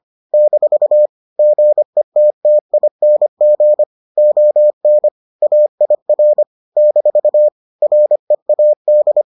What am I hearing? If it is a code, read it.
Morse code, heard as a keyed tone, about 25 wpm, 600 Hz. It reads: - GETTING ON AIR - READ